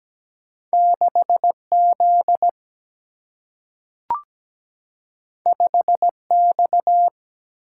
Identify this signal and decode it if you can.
Morse code, heard as a keyed tone, about 17 wpm, 700 Hz. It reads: 6Z 5X
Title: Single-Letter Single-Number 17wpm